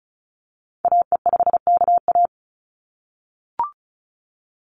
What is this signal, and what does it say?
Morse code, heard as a keyed tone, about 36 wpm, 700 Hz. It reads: AE5KA
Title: NA SOTA Activator Callsigns 35wpm